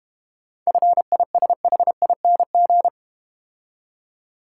Morse code, heard as a keyed tone, about 30 wpm, 700 Hz. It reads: FISHING
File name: Easy Words - Speed-Racing with Hints - 25wpm